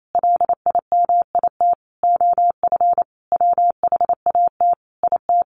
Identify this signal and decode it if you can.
Morse code, heard as a keyed tone, about 28 wpm, 700 Hz. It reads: LIMIT OF WHAT IT
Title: Sentences from Top 1500 Words 28wpm